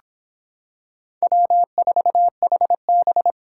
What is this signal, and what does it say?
Morse code, heard as a keyed tone, about 26 wpm, 700 Hz. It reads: W4HB